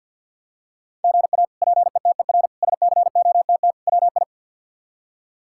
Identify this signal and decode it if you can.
Morse code, heard as a keyed tone, about 52 wpm, 700 Hz. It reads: GA PETER SCOTT WI
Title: K1USN SST QSO Practice - 50wpm